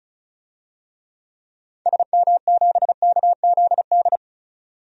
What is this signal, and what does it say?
Morse code, heard as a keyed tone, about 36 wpm, 700 Hz. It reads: SM7KZD
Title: DXCC-100 Call Signs - 6 Characters - 35wpm